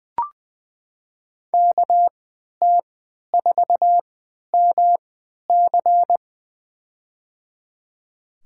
Morse code, heard as a keyed tone, about 20 wpm, spaced as for 11 wpm, 700 Hz. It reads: KT4MC
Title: USA Call Signs - 5 Characters - 3x Character Spacing - 20wpm